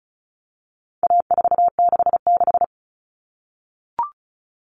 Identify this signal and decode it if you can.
Morse code, heard as a keyed tone, about 34 wpm, 700 Hz. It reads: A466
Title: Common Cut Numbers - 4-Digit 35wpm